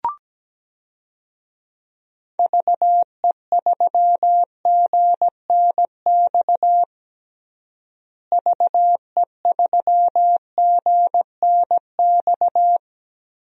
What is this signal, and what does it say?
Morse code, heard as a keyed tone, about 17 wpm, 700 Hz. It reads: VE3GNX VE3GNX
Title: DXCC-100 Call Signs - 6 Characters - Repeated 2x - 17wpm